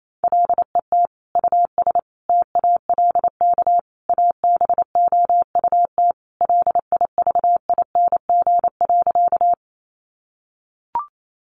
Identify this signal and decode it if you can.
Morse code, heard as a keyed tone, about 28 wpm, 700 Hz. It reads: LET US TALK ABOUT LIVING.
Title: Sentences from Top 1100 Words 28wpm